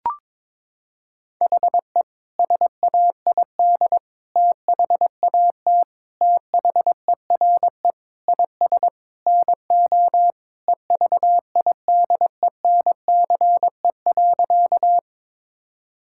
Morse code, heard as a keyed tone, about 22 wpm, 700 Hz. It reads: HE SAID THAT THERE IS NO EVIDENCE.
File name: practice / 22wpm / Sentences with 4-Word Phrases - 22wpm